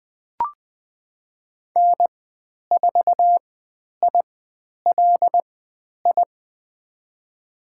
Morse code, heard as a keyed tone, about 20 wpm, spaced as for 10 wpm, 700 Hz. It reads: N4ILI
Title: 102 Lesson 17g - Letter u - Call Signs 20wpm